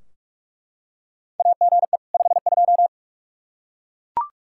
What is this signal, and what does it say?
Morse code, heard as a keyed tone, about 43 wpm, 700 Hz. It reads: AGE 52
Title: Practice QSO - Exchange 3 - More Info - 45wpm